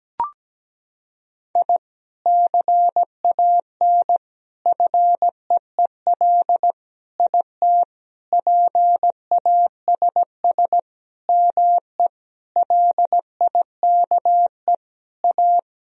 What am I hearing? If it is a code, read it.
Morse code, heard as a keyed tone, about 17 wpm, 700 Hz. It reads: I CAN FEEL IT PASS ME LIKE A
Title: Sentences from Top 1000 Words - Review Part 2 - 17wpm